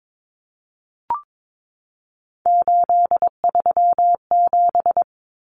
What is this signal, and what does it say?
Morse code, heard as a keyed tone, about 22 wpm, 700 Hz. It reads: 837
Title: Numbers - 3-Digit 22wpm